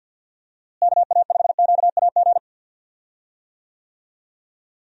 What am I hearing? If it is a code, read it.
Morse code, heard as a keyed tone, about 52 wpm, 700 Hz. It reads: KA5QAG
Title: USA Call Signs - 6 Characters - Repeated 3x - 50wpm